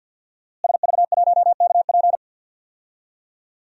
Morse code, heard as a keyed tone, about 51 wpm, 700 Hz. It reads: SV1KP
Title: DXCC-100 Call Signs - 5 Characters - Repeated 2x - 50wpm